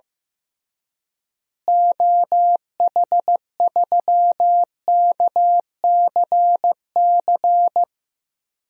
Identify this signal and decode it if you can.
Morse code, heard as a keyed tone, about 15 wpm, 700 Hz. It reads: OH3KCC
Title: DXCC-100 Call Signs - 6 Characters - Repeated 3x - 15wpm